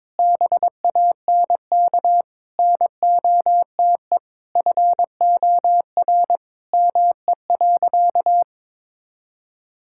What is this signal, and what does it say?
Morse code, heard as a keyed tone, about 21 wpm, 700 Hz. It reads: BANK NOTE FOR ME.
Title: Sentences from Top 1000 Words - Review Part 6 - 22wpm